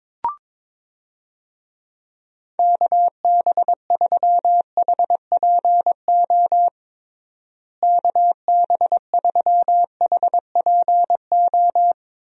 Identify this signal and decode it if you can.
Morse code, heard as a keyed tone, about 22 wpm, 700 Hz. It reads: KB3HPO KB3HPO
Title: USA Call Signs - 6 Characters - Repeated 3x - 22wpm